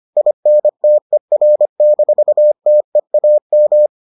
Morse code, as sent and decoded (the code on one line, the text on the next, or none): .. -. - . .-. -....- - . .- --
INTER-TEAM